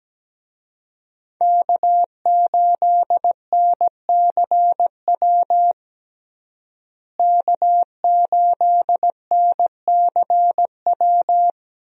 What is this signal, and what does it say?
Morse code, heard as a keyed tone, about 17 wpm, 700 Hz. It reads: K8NCW K8NCW
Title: USA Call Signs - 5 Characters - Repeated 2x - 17wpm